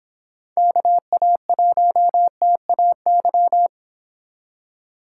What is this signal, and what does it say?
Morse code, heard as a keyed tone, about 26 wpm, 700 Hz. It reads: KA1TAY